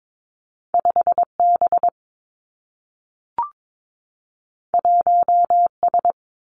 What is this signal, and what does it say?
Morse code, heard as a keyed tone, about 22 wpm, 700 Hz. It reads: 5 B 1 S
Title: Two Challenging Letter-Number Combinations 22wpm